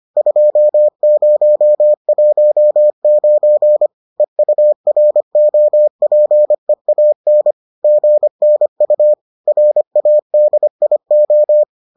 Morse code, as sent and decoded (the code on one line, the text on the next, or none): ..--- ----- .---- ----. / . ..- .-. --- .--. .- -. / --. -. ..- / .-. .- -.. .. ---
2019 EUROPEAN GNU RADIO